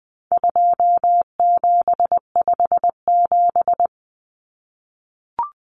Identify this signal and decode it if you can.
Morse code, heard as a keyed tone, about 20 wpm, 700 Hz. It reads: 2757
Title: Common Cut Numbers - 4-Digit 20wpm